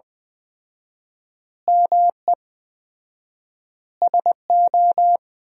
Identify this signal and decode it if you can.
Morse code, heard as a keyed tone, about 20 wpm, 700 Hz. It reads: ME SO
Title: 4x Word-Spacing - Sentences from Top 400 Words - 20wpm